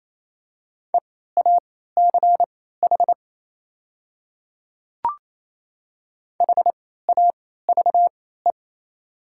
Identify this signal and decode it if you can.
Morse code, heard as a keyed tone, about 28 wpm, spaced as for 16 wpm, 700 Hz. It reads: EACH HAVE
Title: 3x Character Spacing - Top 100 Words - 28wpm